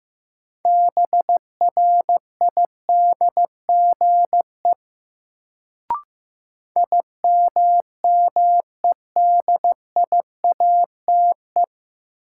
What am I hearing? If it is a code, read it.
Morse code, heard as a keyed tone, about 15 wpm, 700 Hz. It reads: BRIDGE IMMEDIATE
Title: Sentences from Top 1200 Words - New Word Drill Alt - 15wpm